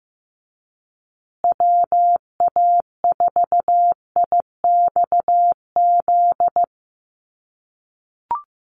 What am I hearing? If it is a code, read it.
Morse code, heard as a keyed tone, about 15 wpm, 700 Hz. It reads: WA4IXZ
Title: USA Call Signs Repeated 3x 15wpm